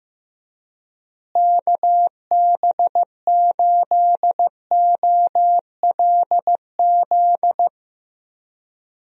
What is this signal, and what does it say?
Morse code, heard as a keyed tone, about 15 wpm, 700 Hz. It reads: KB8OLZ
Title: USA Call Signs - 6 Characters - Repeated 3x - 15wpm